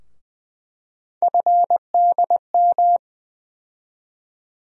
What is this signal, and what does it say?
Morse code, heard as a keyed tone, about 20 wpm, 700 Hz. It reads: FDM
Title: USA Call Sign Postfixes - 3 Letters 20wpm